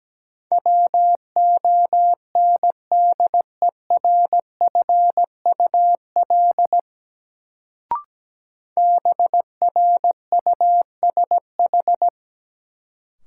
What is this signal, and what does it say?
Morse code, heard as a keyed tone, about 17 wpm, 700 Hz. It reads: WONDERFUL BRUSH
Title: Sentences from Top 1100 Words - New Word Drill - 17wpm